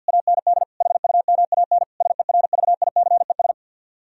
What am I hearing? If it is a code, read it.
Morse code, heard as a keyed tone, about 65 wpm, spaced as for 57 wpm, 700 Hz. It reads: AND HUMAN SERVICES
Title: Speed-Racing - Sets of 5 Words 50wpm